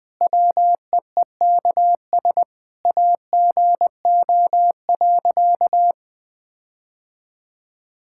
Morse code, heard as a keyed tone, about 21 wpm, 700 Hz. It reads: WEEKS AGO.
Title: Sentences with 5-Word Phrases - 20wpm